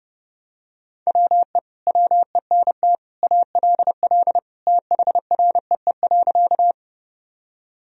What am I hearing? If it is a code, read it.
Morse code, heard as a keyed tone, about 30 wpm, 700 Hz. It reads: WE WENT ALL THREE.
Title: Sentences from Top 1000 Words - Review Part 1 - 30wpm